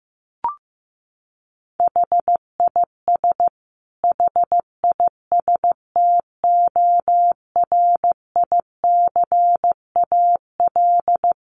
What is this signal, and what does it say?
Morse code, heard as a keyed tone, about 15 wpm, 700 Hz. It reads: HIS HISTORICAL
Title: Sentences from Top 1300 Words 15wpm